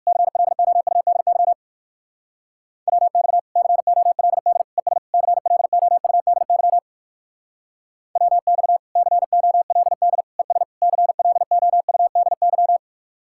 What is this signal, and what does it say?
Morse code, heard as a keyed tone, about 56 wpm, 700 Hz. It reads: CLOUDY WX COLD ES CLOUDY WX COLD ES CLOUDY
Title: Speed-Racing Common QSO Phrases Encore - 45wpm